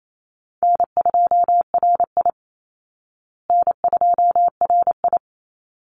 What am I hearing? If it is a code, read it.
Morse code, heard as a keyed tone, about 28 wpm, 700 Hz. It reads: N2RI N2RI
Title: Call Signs Repeated 3x - 28wpm